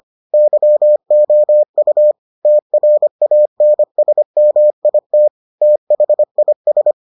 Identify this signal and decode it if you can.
Morse code, heard as a keyed tone, about 25 wpm, 600 Hz. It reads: YOU TRANSMIT THIS